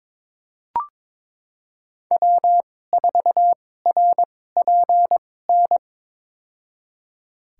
Morse code, heard as a keyed tone, about 22 wpm, spaced as for 16 wpm, 700 Hz. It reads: W4RPN
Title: USA Call Signs - 5 Characters - 2x Character Spacing - 22wpm